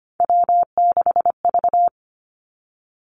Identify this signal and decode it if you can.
Morse code, heard as a keyed tone, about 25 wpm, 700 Hz. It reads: W6V
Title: Three Challenging Letter-Number Combinations 25wpm